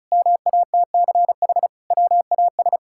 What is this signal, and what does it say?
Morse code, heard as a keyed tone, about 34 wpm, 700 Hz. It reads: MATCH WAS
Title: Sentences from Top 1600 Words 35wpm